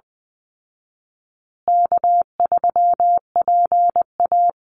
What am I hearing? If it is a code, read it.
Morse code, heard as a keyed tone, about 20 wpm, 700 Hz. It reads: K3PA
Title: Call Signs Repeated 2x - 20wpm